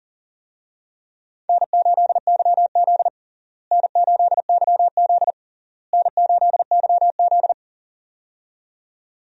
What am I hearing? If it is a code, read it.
Morse code, heard as a keyed tone, about 40 wpm, 700 Hz. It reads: N8YZ N8YZ N8YZ
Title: USA Call Signs - 4 Characters - Repeated 3x - 40wpm